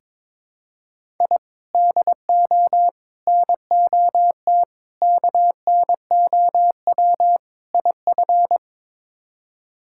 Morse code, heard as a keyed tone, about 22 wpm, 700 Hz. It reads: I DO NOT KNOW IF
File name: practice / 22wpm / Build A Phrase - 5-Word Phrases - 22wpm